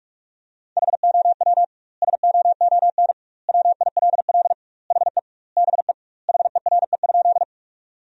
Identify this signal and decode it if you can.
Morse code, heard as a keyed tone, about 41 wpm, 700 Hz. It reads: HOW SOON WILL HE BE HERE?
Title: 2x Word-Spacing - Sentences from Top 400 Words - 45wpm